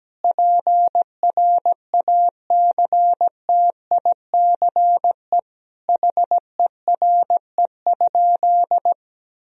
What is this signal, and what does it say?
Morse code, heard as a keyed tone, about 17 wpm, 700 Hz. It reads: PRACTICE HERE?
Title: Sentences from Top 1000 Words - Review Part 4 - 17wpm